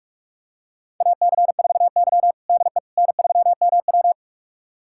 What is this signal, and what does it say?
Morse code, heard as a keyed tone, about 41 wpm, 700 Hz. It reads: AC4Y DE N3MW
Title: USA Callsign DE Callsign - 4 Characters - 45wpm